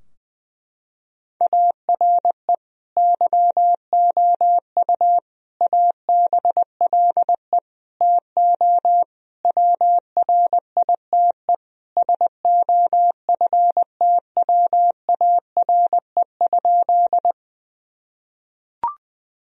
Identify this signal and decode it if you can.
Morse code, heard as a keyed tone, about 21 wpm, 700 Hz. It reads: ARE YOU ABLE TO WRITE SOFTWARE?
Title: Sentences from Top 1000 Words - Review Part 3 - 20wpm